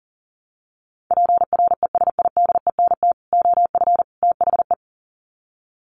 Morse code, heard as a keyed tone, about 40 wpm, 700 Hz. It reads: PRESIDENT OF THE
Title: Sets of 3 Words 40wpm